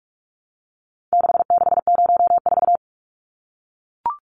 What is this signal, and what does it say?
Morse code, heard as a keyed tone, about 44 wpm, 700 Hz. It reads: 6604
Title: Numbers - 4-Digit 45wpm